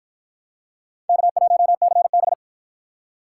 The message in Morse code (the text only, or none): K1CB